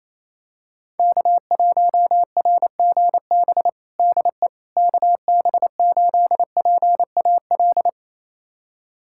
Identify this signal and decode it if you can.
Morse code, heard as a keyed tone, about 27 wpm, 700 Hz. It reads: K1RGB DE KB8PAL
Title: USA Callsign DE Callsign 28wpm